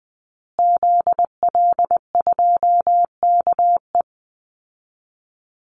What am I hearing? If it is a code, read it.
Morse code, heard as a keyed tone, about 20 wpm, 700 Hz. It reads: ZL2KE
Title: Call Signs Repeated 2x - 20wpm